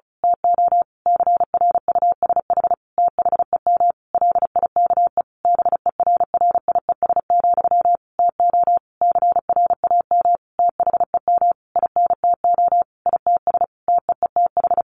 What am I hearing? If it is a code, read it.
Morse code, heard as a keyed tone, about 34 wpm, 700 Hz. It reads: TO CRUSH THEM LIKE BERRIES, TO CRAM THEM INTO ITS TEETH
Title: A Sound Of Thunder 35wpm